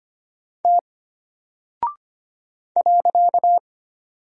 Morse code, heard as a keyed tone, about 25 wpm, 700 Hz. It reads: T .
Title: Single Character (Letter:Number:Punctuation:Prosign) Alt 25wpm